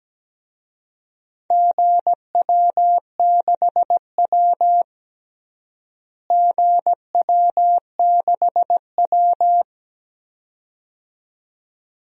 Morse code, heard as a keyed tone, about 17 wpm, 700 Hz. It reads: GW6W GW6W
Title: DXCC-100 Call Signs - 4 Characters - Repeated 2x - 17wpm